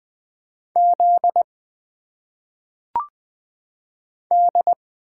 Morse code, heard as a keyed tone, about 20 wpm, 700 Hz. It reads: Z D
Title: Single Letters - Phonetic Pronunciation 20wpm